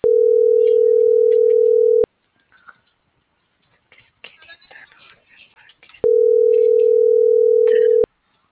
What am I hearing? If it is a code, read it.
Ringback tone (North America), 440 + 480 Hz.